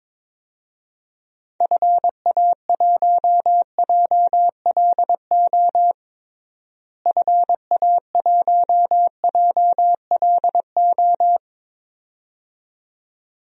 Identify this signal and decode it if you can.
Morse code, heard as a keyed tone, about 22 wpm, 700 Hz. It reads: FA1JLO FA1JLO
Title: DXCC-100 Call Signs - 6 Characters - Repeated 2x - 22wpm